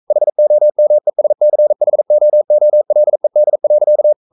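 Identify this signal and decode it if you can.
Morse code, heard as a keyed tone, about 42 wpm, 600 Hz. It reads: HOMESCHOOLED.
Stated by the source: Morse code sending of an English novel at 42 wpm.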